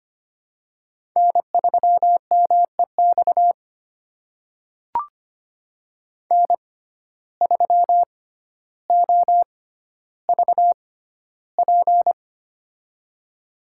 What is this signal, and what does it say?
Morse code, heard as a keyed tone, about 25 wpm, spaced as for 13 wpm, 700 Hz. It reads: N3MEX N3OVP
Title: USA Call Signs - 5 Characters - 6x Character Spacing - 25wpm